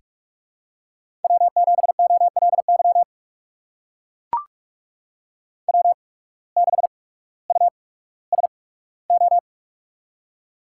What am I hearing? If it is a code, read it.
Morse code, heard as a keyed tone, about 44 wpm, spaced as for 19 wpm, 700 Hz. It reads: W7OLY W6USO
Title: USA Call Signs - 5 Characters - 8x Character Spacing - 45wpm